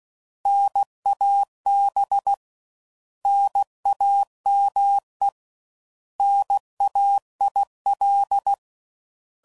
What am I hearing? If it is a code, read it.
Morse code, heard as a keyed tone, about 16 wpm, 800 Hz. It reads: NAB NAME NAIL
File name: Prefixes / By First Letter / Ns